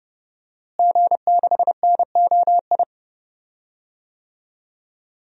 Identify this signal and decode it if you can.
Morse code, heard as a keyed tone, about 30 wpm, 700 Hz. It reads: G6NOI
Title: DXCC-100 Call Signs - 5 Characters - Repeated 3x - 30wpm